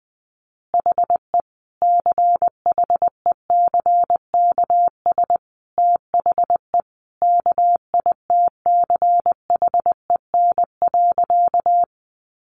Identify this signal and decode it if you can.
Morse code, heard as a keyed tone, about 20 wpm, 700 Hz. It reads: HE CHECKS THE KITCHEN.
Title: Easy Phrases 20wpm